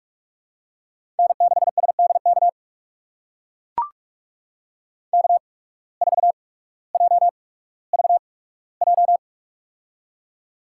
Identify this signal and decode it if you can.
Morse code, heard as a keyed tone, about 42 wpm, spaced as for 19 wpm, 700 Hz. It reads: N6SDK K4JVJ
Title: USA Call Signs - 5 Characters - 8x Character Spacing - 45wpm